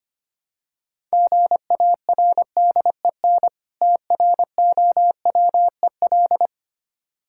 Morse code, heard as a keyed tone, about 26 wpm, 700 Hz. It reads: GARDEN TROWEL
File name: practice / 25wpm / Difficult Words - Speed-Racing with Hints - 25wpm